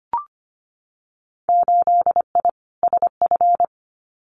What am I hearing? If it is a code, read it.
Morse code, heard as a keyed tone, about 25 wpm, 700 Hz. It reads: 8I SF
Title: Winter Field Day Exchanges Encore 25wpm